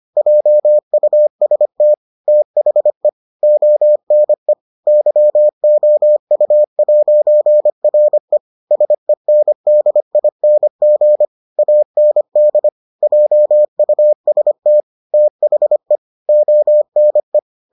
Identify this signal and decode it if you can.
Morse code, heard as a keyed tone, about 25 wpm, 600 Hz. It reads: JUST THE ONE YOU'RE SENDING AND JUST THE ONE